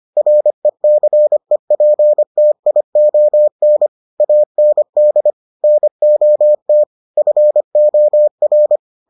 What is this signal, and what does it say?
Morse code, heard as a keyed tone, about 25 wpm, 600 Hz. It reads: RECEPTION AND NOT FOR